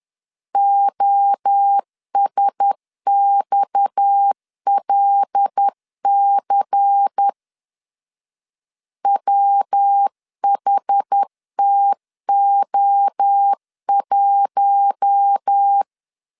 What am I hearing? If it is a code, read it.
Morse code, heard as a keyed tone, about 10 wpm, 800 Hz. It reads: OSXLC WHTO1
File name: LEZIONI / 6 PKX0 / 6 PKX0 10wpm 2707